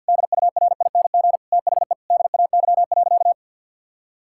Morse code, heard as a keyed tone, about 52 wpm, 700 Hz. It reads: DURING THE DAY.
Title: Sentences from Top 1000 Words - Review Part 6 - 50wpm